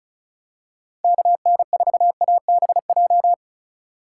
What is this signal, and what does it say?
Morse code, heard as a keyed tone, about 36 wpm, 700 Hz. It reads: KN4ABJ
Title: USA Call Signs - 6 Characters - 3x Character Spacing - 35wpm